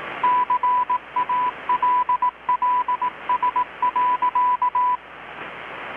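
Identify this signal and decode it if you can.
Morse code, heard as a keyed tone, about 18 wpm, 990 Hz, with radio noise. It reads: CALLS.
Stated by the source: Morse from KPH's Night of Nights XX maritime radiotelegraphy event, received in Chiapas, Mexico about 3,800 km away on a Kenwood R-600 with a grounded 10 m long-wire antenna.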